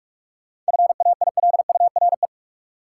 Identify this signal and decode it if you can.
Morse code, heard as a keyed tone, about 46 wpm, 700 Hz. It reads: FAILURE
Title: Sentences from Top 900 Words - New Word Drill - 45wpm